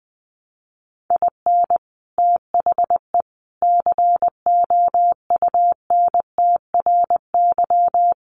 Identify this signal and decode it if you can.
Morse code, heard as a keyed tone, about 20 wpm, 700 Hz. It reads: IN THE COUNTRY
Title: Sets of 3 Words 20wpm